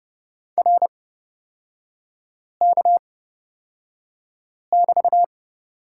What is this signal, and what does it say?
Morse code, heard as a keyed tone, about 30 wpm, spaced as for 5 wpm, 700 Hz. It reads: RK<BT>
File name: Lesson 38e - Letter q - Three Characters 30wpm